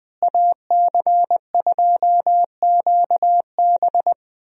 Morse code, heard as a keyed tone, about 20 wpm, 700 Hz. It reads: AC2QB
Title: USA Call Signs - 5 Characters - Repeated 2x - 20wpm